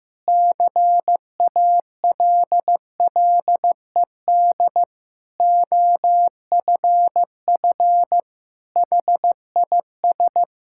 Morse code, heard as a keyed tone, about 15 wpm, 700 Hz. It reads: CALLED OFF HIS
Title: Pangram Challenge v2 - 15wpm